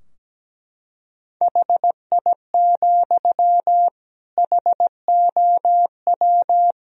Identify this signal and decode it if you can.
Morse code, heard as a keyed tone, about 17 wpm, 700 Hz. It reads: HI, HOW
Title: Sentences from Top 1000 Words - Review Part 3 - 17wpm